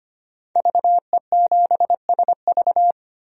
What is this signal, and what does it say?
Morse code, heard as a keyed tone, about 25 wpm, 700 Hz. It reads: VE7SV
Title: DXCC-100 Call Signs - 5 Characters - 25wpm